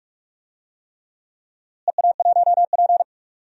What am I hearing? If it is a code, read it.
Morse code, heard as a keyed tone, about 46 wpm, 700 Hz. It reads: EA1P
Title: DXCC-100 Call Signs - 4 Characters - Repeated 3x - 45wpm